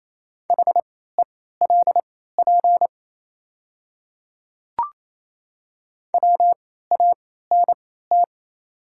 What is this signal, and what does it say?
Morse code, heard as a keyed tone, about 28 wpm, spaced as for 16 wpm, 700 Hz. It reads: HELP WANT
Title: 3x Character Spacing - Top 300 Words - 28wpm